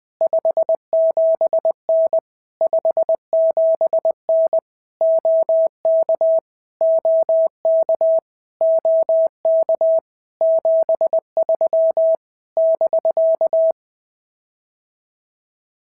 Morse code, heard as a keyed tone, about 20 wpm, 650 Hz. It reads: 57N 57N OK OK OK 73 <BK>
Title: POTA QSO Practice - 20wpm